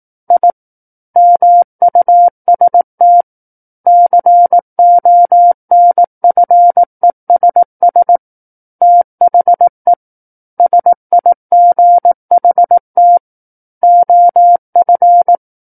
Morse code, heard as a keyed tone, about 18 wpm, 700 Hz. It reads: I MUST CONFESS THE SIGHT OF
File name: 18 WPM